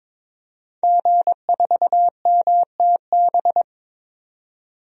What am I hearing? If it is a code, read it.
Morse code, heard as a keyed tone, about 21 wpm, 700 Hz. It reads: G4MTB
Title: DXCC-100 Call Signs - 5 Characters - 22wpm